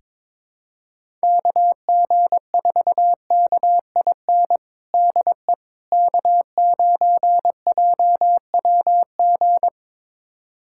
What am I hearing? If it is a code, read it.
Morse code, heard as a keyed tone, about 22 wpm, 700 Hz. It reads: KG4KIN DE K9JWG
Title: USA Callsign DE Callsign 22wpm